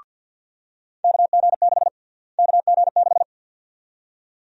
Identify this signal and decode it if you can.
Morse code, heard as a keyed tone, about 52 wpm, 700 Hz. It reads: KG6 KG6
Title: USA Call Sign Prefixes - Repeated 2x - 50wpm